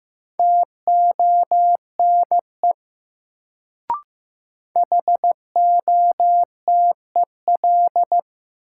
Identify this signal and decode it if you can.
Morse code, heard as a keyed tone, about 15 wpm, 700 Hz. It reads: TONE HOTEL
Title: Sentences from Top 1600 Words - New Word Drill Alt - 15wpm